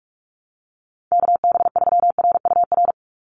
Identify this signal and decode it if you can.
Morse code, heard as a keyed tone, about 44 wpm, 700 Hz. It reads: KB3RUR
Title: USA Call Signs 45wpm